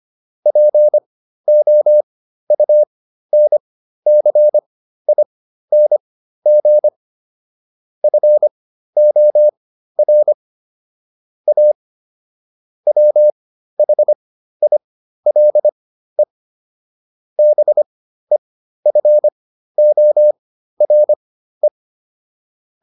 Morse code, heard as a keyed tone, about 25 wpm, spaced as for 13 wpm, 600 Hz. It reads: POUNCING FOR A WHILE BEFORE